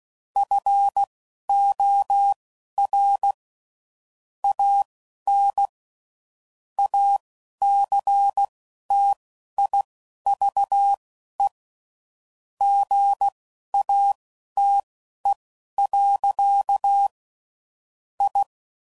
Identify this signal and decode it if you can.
Morse code, heard as a keyed tone, about 16 wpm, spaced as for 11 wpm, 800 Hz. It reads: FOR AN ACTIVE GATE. I
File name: CW Code Groups @10WPM/Technical Subjects